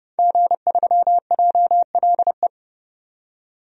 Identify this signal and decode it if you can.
Morse code, heard as a keyed tone, about 30 wpm, 700 Hz. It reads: G3JLE
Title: DXCC-100 Call Signs - 5 Characters - 30wpm